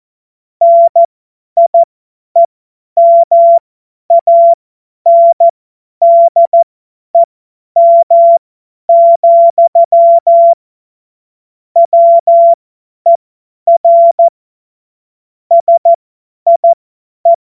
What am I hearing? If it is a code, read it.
Morse code, heard as a keyed tone, about 14 wpm, spaced as for 10 wpm, 680 Hz. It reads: NIEMANDEM, WER SIE
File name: Morsekurs / Witze / Witz 20. Dezember 2022 50 F70